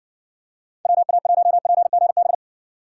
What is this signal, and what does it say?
Morse code, heard as a keyed tone, about 60 wpm, 700 Hz. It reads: WA1PGB